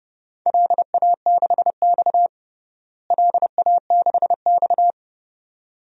Morse code, heard as a keyed tone, about 30 wpm, 700 Hz. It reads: LA6X LA6X
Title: DXCC-100 Call Signs - 4 Characters - Repeated 2x - 30wpm